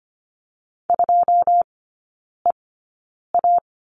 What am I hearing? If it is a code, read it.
Morse code, heard as a keyed tone, about 25 wpm, spaced as for 9 wpm, 700 Hz. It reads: 2EA
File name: Lesson 14e - Number 2 - Three Characters 25wpm